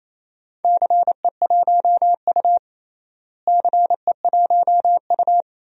Morse code, heard as a keyed tone, about 28 wpm, 700 Hz. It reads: CE1U CE1U
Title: DXCC-100 Call Signs - 4 Characters - Repeated 3x - 28wpm